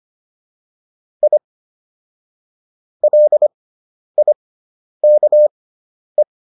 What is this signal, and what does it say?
Morse code, heard as a keyed tone, about 25 wpm, spaced as for 10 wpm, 600 Hz. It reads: I LIKE